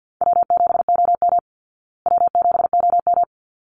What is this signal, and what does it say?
Morse code, heard as a keyed tone, about 50 wpm, 700 Hz. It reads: W7OM W7OM
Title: Call Signs Repeated 3x - 50wpm